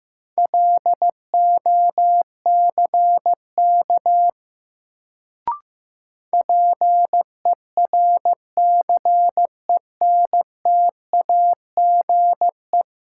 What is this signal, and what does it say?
Morse code, heard as a keyed tone, about 15 wpm, 700 Hz. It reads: LOCK PERCENTAGE
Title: Sentences from Top 1300 Words - New Word Drill - 15wpm